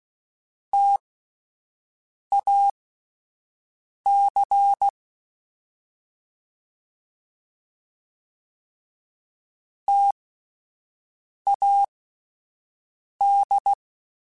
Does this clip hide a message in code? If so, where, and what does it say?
Yes: Morse code from 0.7 to 13.7 s, heard as a keyed tone, about 16 wpm, spaced as for 5 wpm, 800 Hz. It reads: TAC TAD